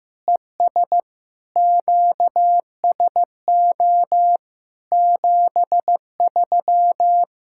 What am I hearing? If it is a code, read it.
Morse code, heard as a keyed tone, about 15 wpm, 700 Hz. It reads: ES QSO 73
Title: Practice QSO - Exchange 4 - Ending - 15wpm